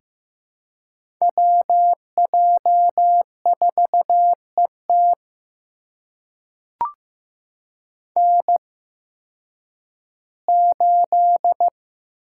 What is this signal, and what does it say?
Morse code, heard as a keyed tone, about 15 wpm, spaced as for 10 wpm, 700 Hz. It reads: WJ4ET N8
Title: USA Call Signs - 5 Characters - 8x Character Spacing - 15wpm